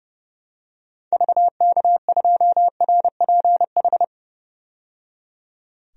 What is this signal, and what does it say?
Morse code, heard as a keyed tone, about 32 wpm, 700 Hz. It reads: VK2RPH
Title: DXCC-100 Call Signs - 6 Characters - 30wpm